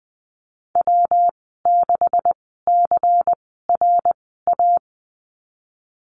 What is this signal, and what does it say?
Morse code, heard as a keyed tone, about 20 wpm, spaced as for 14 wpm, 700 Hz. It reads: W6CRA
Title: 167 Lesson 27g - Number 7 - Call Signs 20wpm